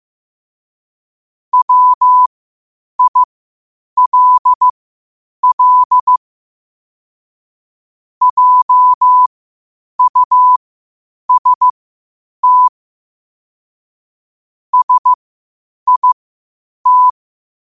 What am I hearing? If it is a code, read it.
Morse code, heard as a keyed tone, about 15 wpm, spaced as for 8 wpm, 1000 Hz. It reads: WILL JUST SIT
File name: qotdfiles / 8wpm